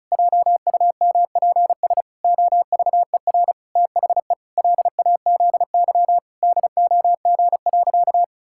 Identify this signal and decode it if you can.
Morse code, heard as a keyed tone, about 34 wpm, 700 Hz. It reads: JUMPS OVER THE LAZY DOG.
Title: Pangram Challenge v2 - 35wpm